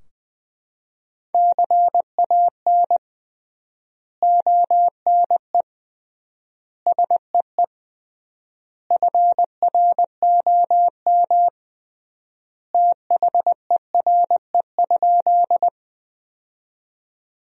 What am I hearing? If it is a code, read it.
Morse code, heard as a keyed tone, about 21 wpm, 700 Hz. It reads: CAN ONE SEE FROM THERE?
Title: 3x Word-Spacing - Sentences from Top 100 Words - 20wpm